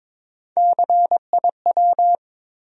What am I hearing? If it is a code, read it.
Morse code, heard as a keyed tone, about 21 wpm, 700 Hz. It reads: CIW
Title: USA Call Sign Postfixes - 3 Letters 22wpm